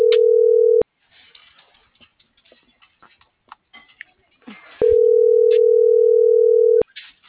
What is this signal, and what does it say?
Ringback tone (North America), 440 + 480 Hz.